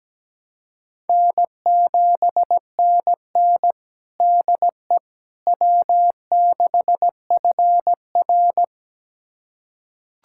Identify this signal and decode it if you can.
Morse code, heard as a keyed tone, about 17 wpm, 700 Hz. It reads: N7NN DE W6FR
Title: USA Callsign DE Callsign - 4 Characters - 17wpm